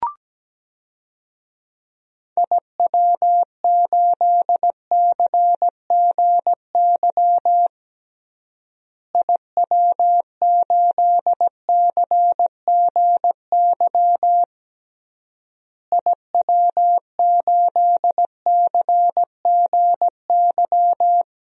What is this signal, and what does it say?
Morse code, heard as a keyed tone, about 17 wpm, 700 Hz. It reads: IW8CGY IW8CGY IW8CGY
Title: DXCC-100 Call Signs - 6 Characters - Repeated 3x - 17wpm